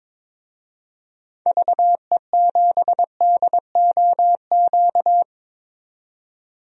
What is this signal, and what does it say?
Morse code, heard as a keyed tone, about 22 wpm, 700 Hz. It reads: VE7DOQ
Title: DXCC-100 Call Signs - 6 Characters - 22wpm